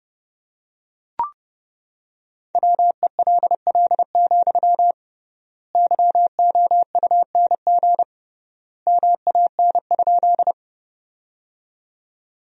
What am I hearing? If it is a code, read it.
Morse code, heard as a keyed tone, about 32 wpm, 700 Hz. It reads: WELL, YOUNG MAN?
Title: 3x Word-Spacing - Sentences from Top 400 Words - 30wpm